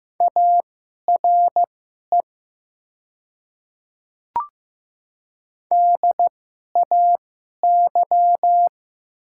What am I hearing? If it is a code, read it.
Morse code, heard as a keyed tone, about 15 wpm, spaced as for 11 wpm, 700 Hz. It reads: ARE DAY
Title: 2x Character Spacing - Top 100 Words - 15wpm